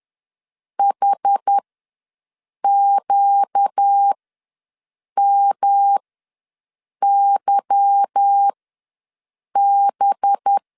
Morse code, heard as a keyed tone, about 10 wpm, spaced as for 6 wpm, 800 Hz. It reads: HQMYB
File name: LEZIONI / 8 YJ45 / 8 Y ws 10wpm 3952